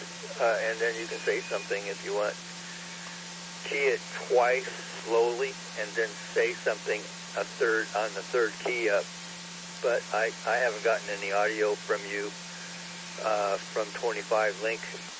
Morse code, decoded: none

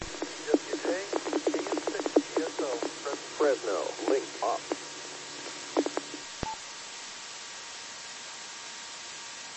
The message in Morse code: none

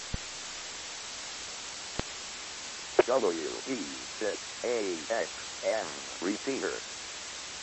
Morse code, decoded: none

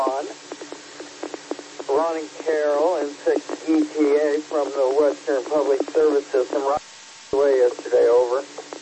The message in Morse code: none

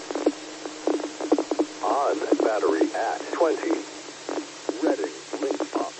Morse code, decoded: none

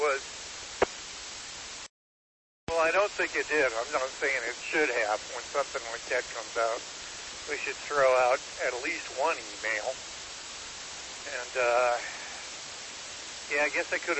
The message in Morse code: none

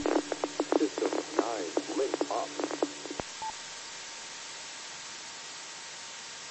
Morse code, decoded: none

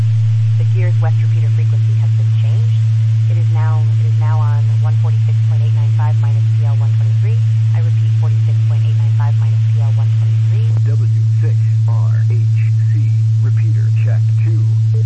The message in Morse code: none

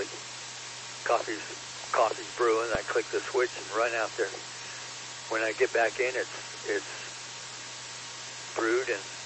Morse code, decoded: none

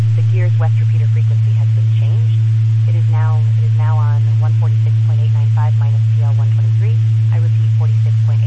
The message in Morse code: none

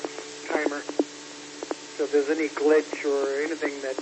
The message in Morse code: none

